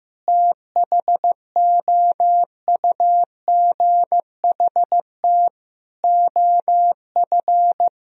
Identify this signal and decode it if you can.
Morse code, heard as a keyed tone, about 16 wpm, 700 Hz. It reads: THOUGHT OF
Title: Sentences from Top 1000 Words - Review Part 2 - 15wpm